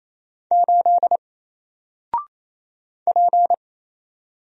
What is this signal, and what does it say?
Morse code, heard as a keyed tone, about 28 wpm, 700 Hz. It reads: 8 P
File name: Single-Letter-Number-28wpm